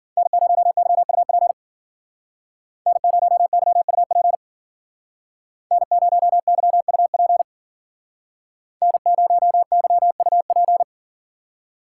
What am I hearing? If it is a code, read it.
Morse code, heard as a keyed tone, about 50 wpm, 700 Hz. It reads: N0YUP N0YUP N0YUP N0YUP
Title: Speed Racing - USA 1x3 Call Signs 40wpm